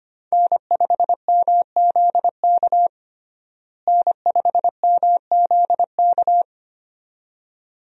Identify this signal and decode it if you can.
Morse code, heard as a keyed tone, about 25 wpm, 700 Hz. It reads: N5MZK N5MZK
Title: USA Call Signs - 5 Characters - Repeated 2x - 25wpm